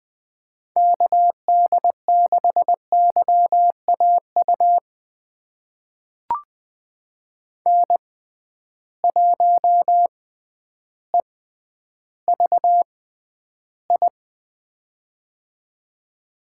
Morse code, heard as a keyed tone, about 20 wpm, spaced as for 11 wpm, 700 Hz. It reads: KD6YAU N1EVI
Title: USA Call Signs - 6x Character Spacing - 20wpm